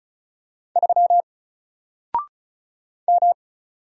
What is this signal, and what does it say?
Morse code, heard as a keyed tone, about 36 wpm, 700 Hz. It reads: 3 M
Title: Single Character (Letter:Number:Punctuation:Prosign) 35wpm